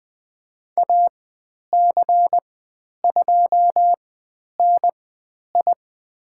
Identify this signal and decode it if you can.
Morse code, heard as a keyed tone, about 20 wpm, spaced as for 10 wpm, 700 Hz. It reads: AC2NI
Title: 102 Lesson 17g - Letter u - Call Signs 20wpm